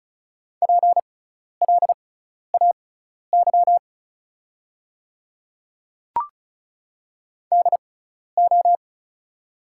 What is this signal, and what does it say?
Morse code, heard as a keyed tone, about 34 wpm, spaced as for 12 wpm, 700 Hz. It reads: PLAY DO
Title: 6x Character Spacing - Top 200 Words - 35wpm